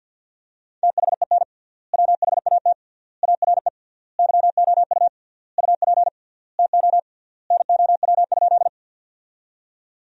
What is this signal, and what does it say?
Morse code, heard as a keyed tone, about 50 wpm, 700 Hz. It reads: THEN WHAT ARE YOU UP TO NOW?
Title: 3x Word-Spacing - Sentences from Top 100 Words - 50wpm